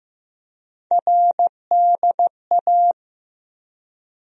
Morse code, heard as a keyed tone, about 15 wpm, 700 Hz. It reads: RDA
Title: USA Call Sign Postfixes - 3 Letters 15wpm